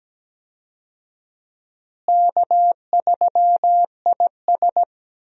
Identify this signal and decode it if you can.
Morse code, heard as a keyed tone, about 17 wpm, 700 Hz. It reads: K3IS
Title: USA Call Signs - 4 Characters - Repeated 2x - 17wpm